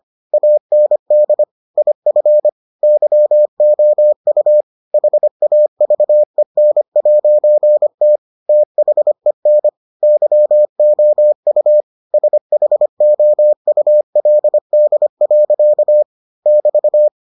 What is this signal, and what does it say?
Morse code, heard as a keyed tone, about 25 wpm, 600 Hz. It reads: AND IF YOU HAVEN'T THEN YOU SHOULD. <BT>